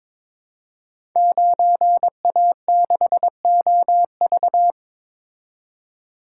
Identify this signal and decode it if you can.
Morse code, heard as a keyed tone, about 22 wpm, 690 Hz. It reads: 9A6OV
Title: Code Groups - 5 Letter:Numbers - 22wpm